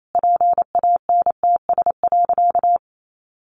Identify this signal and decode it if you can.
Morse code, heard as a keyed tone, about 28 wpm, 700 Hz. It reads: PANTS.
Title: A Sound Of Thunder 28wpm